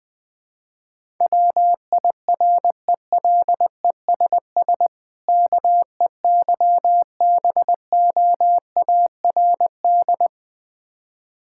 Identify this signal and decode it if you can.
Morse code, heard as a keyed tone, about 20 wpm, 690 Hz. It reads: WIRELESS KEYBOARD
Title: Difficult Words with Hints - 20wpm